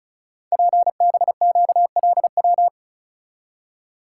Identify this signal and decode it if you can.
Morse code, heard as a keyed tone, about 36 wpm, 690 Hz. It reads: PBQLW
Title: Code Groups - 5 Letters - 35wpm